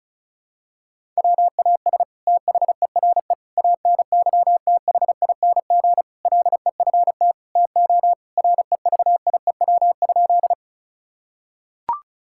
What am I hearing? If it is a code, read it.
Morse code, heard as a keyed tone, about 34 wpm, 700 Hz. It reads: WAS THERE ANYTHING LEFT TO REVIEW?
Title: Sentences from Top 1000 Words - Review Part 6 - 35wpm